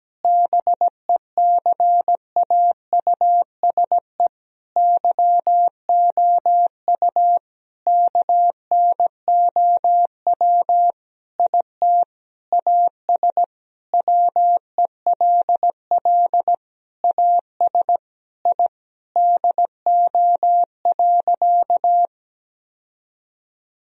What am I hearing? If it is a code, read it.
Morse code, heard as a keyed tone, about 17 wpm, 700 Hz. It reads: BECAUSE YOU KNOW IT AS WELL AS I DO.
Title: Sentences from Top 1000 Words - Review Part 3 - 17wpm